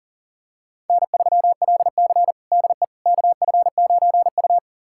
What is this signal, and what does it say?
Morse code, heard as a keyed tone, about 40 wpm, 700 Hz. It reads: N3LC DE KF9U
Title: USA Callsign DE Callsign - 4 Characters - 40wpm